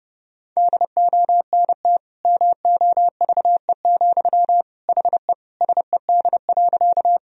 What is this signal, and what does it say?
Morse code, heard as a keyed tone, about 30 wpm, 700 Hz. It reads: DONT MOVE, HE SED.
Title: Sentences from Top 300 Words and Abbreviations 30wpm